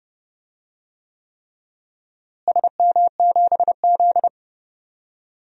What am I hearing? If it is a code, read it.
Morse code, heard as a keyed tone, about 31 wpm, 700 Hz. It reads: SM7Z
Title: DXCC-100 Call Signs - 4 Characters - Repeated 3x - 30wpm